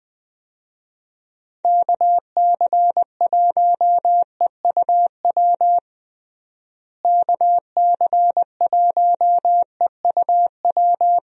Morse code, heard as a keyed tone, about 20 wpm, 700 Hz. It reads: KC1EUW KC1EUW
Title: USA Call Signs - 6 Characters - Repeated 3x - 20wpm